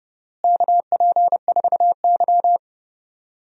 Morse code, heard as a keyed tone, about 30 wpm, 700 Hz. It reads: KP4Y
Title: DXCC-100 Call Signs - 4 Characters - Repeated 3x - 30wpm